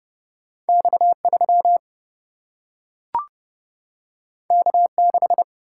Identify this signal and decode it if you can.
Morse code, heard as a keyed tone, about 32 wpm, 700 Hz. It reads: X 3 K 6
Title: Single-Letter Single-Number 30wpm